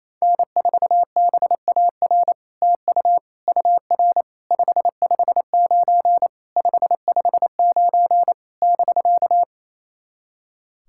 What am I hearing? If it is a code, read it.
Morse code, heard as a keyed tone, about 28 wpm, 700 Hz. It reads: N4BAR TU UR 559 559 <BK>
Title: SOTA QSO Practice 28wpm